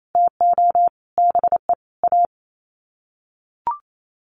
Morse code, heard as a keyed tone, about 28 wpm, 700 Hz. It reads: TO BE A
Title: Sets of 5 Words 28wpm